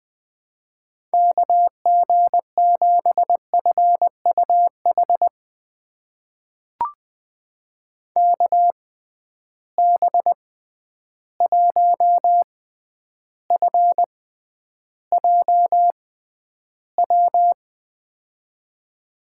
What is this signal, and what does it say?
Morse code, heard as a keyed tone, about 20 wpm, spaced as for 10 wpm, 700 Hz. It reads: KG7FUH KB1FJW
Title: USA Call Signs - 6 Characters - 6x Character Spacing - 20wpm